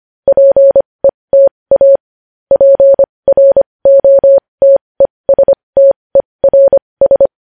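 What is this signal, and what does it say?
Morse code, heard as a keyed tone, about 25 wpm, spaced as for 20 wpm, 550 Hz. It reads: PETA PROTESTERS
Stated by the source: Morse news headlines sent at 20, opening with vvv.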